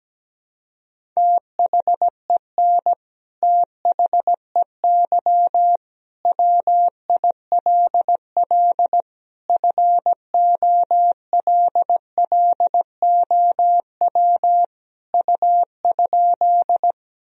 Morse code, heard as a keyed tone, about 17 wpm, 700 Hz. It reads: THEN THEY WILL FOLLOW U?
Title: Sentences from Top 200 Words and Abbreviations 17wpm